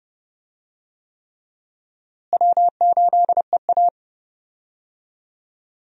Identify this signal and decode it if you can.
Morse code, heard as a keyed tone, about 30 wpm, 700 Hz. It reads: W8EA
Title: USA Call Signs - 4 Characters - Repeated 3x - 30wpm